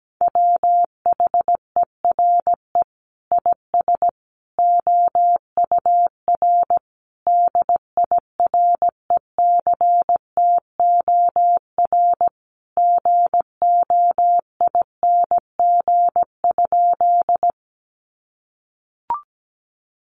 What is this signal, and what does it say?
Morse code, heard as a keyed tone, about 17 wpm, 700 Hz. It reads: WHERE IS OUR DIRECTOR GOING?
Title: Sentences from Top 1300 Words 17wpm